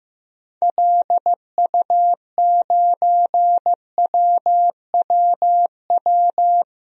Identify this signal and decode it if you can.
Morse code, heard as a keyed tone, about 15 wpm, 700 Hz. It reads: LU9WWW
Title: DXCC-100 Call Signs - 6 Characters - Repeated 2x - 15wpm